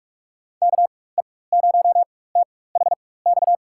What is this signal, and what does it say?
Morse code, heard as a keyed tone, about 43 wpm, spaced as for 21 wpm, 700 Hz. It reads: KE0THX
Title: USA Call Signs - 6 Characters - 4x Character Spacing - 45wpm